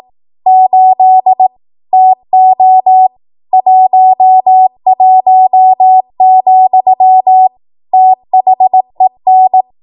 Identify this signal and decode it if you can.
Morse code, heard as a keyed tone, about 18 wpm, 750 Hz. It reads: 8 TO 11, THEN
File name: Archive / 18 WPM / 18WPM